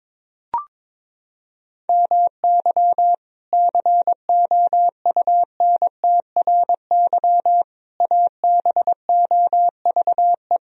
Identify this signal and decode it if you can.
Morse code, heard as a keyed tone, about 21 wpm, 700 Hz. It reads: MY COUNTRY ABOVE
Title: Sentences from Top 1000 Words - Review Part 2 - 22wpm